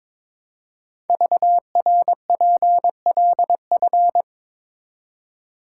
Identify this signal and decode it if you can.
Morse code, heard as a keyed tone, about 22 wpm, 690 Hz. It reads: VRPLF
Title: Code Groups - 5 Letters - 22wpm